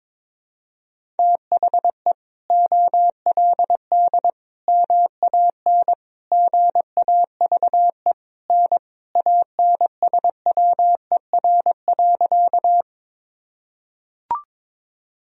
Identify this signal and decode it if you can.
Morse code, heard as a keyed tone, about 22 wpm, 700 Hz. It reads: THE OLD MAN GAVE N ANSWER.
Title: Sentences from Top 500 Words and Abbreviations 22wpm